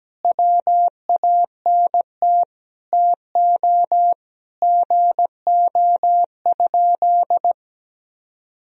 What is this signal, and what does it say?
Morse code, heard as a keyed tone, about 17 wpm, 700 Hz. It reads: WANT TO GO?
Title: Sentences from Top 300 Words and Abbreviations 17wpm